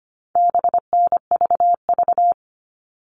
Morse code, heard as a keyed tone, about 25 wpm, 700 Hz. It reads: BNVV
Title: All Cut Numbers - 4-Digit 25wpm